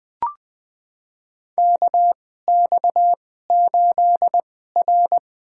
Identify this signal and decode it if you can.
Morse code, heard as a keyed tone, about 20 wpm, spaced as for 15 wpm, 700 Hz. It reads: KX8R
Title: USA Call Signs - 4 Characters - 2x Character Spacing - 20wpm